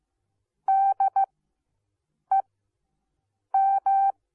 Morse code, heard as a keyed tone, about 15 wpm, spaced as for 6 wpm, 780 Hz. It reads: DEM